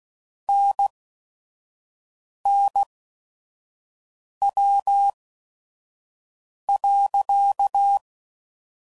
Morse code, heard as a keyed tone, about 16 wpm, spaced as for 5 wpm, 800 Hz. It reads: NNW.
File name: CW Code Groups @5WPM / Weather Sets / Weather 5 12